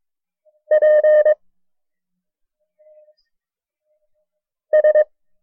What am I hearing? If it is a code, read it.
Morse code, heard as a keyed tone, about 22 wpm, 610 Hz. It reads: P S
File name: FreeFiles / CW-practice / KB6MT-tape1-side2